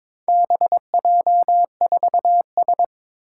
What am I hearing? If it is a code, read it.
Morse code, heard as a keyed tone, about 22 wpm, 700 Hz. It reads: BJ4S
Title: Four Challenging Letter-Number Combinations 22wpm